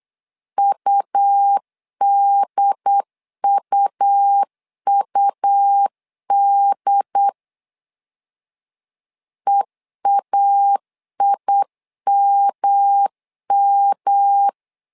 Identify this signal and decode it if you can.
Morse code, heard as a keyed tone, about 8 wpm, 800 Hz. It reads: UDUUD EAIMM
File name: LEZIONI / 3 MWRO / 3 MW 8wpm 2501